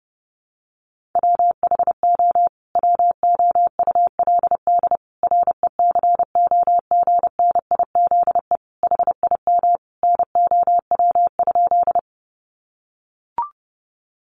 Morse code, heard as a keyed tone, about 28 wpm, 700 Hz. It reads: WHO WOULD RECOGNIZE HIM NOW?
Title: Sentences from Top 1400 Words 30wpm